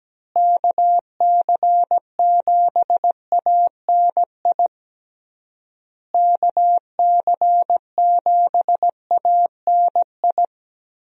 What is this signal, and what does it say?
Morse code, heard as a keyed tone, about 17 wpm, 700 Hz. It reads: KC7ANI KC7ANI
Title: USA Call Signs - 6 Characters - Repeated 3x - 17wpm